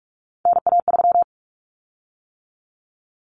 Morse code, heard as a keyed tone, about 44 wpm, 700 Hz. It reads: NA3
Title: Common Cut Numbers - 3-Digit 45wpm